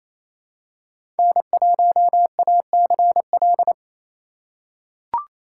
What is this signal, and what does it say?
Morse code, heard as a keyed tone, about 28 wpm, 700 Hz. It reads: N1ACL
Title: USA Call Signs - 5 Characters - 6x Character Spacing - 28wpm